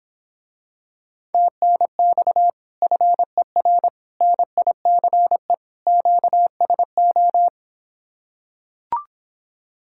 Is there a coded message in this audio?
Yes: Morse code from 1.3 to 7.5 s, heard as a keyed tone, about 26 wpm, 700 Hz. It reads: TNX FER NICE QSO